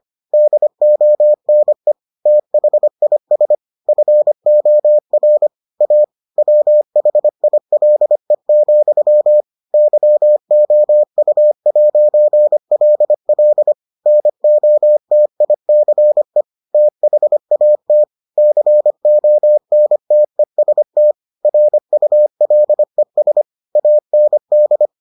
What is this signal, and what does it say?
Morse code, heard as a keyed tone, about 25 wpm, 600 Hz. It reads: DONE THIS FOR A WHILE, YOU'LL NOTICE THAT CONTEST RULES AND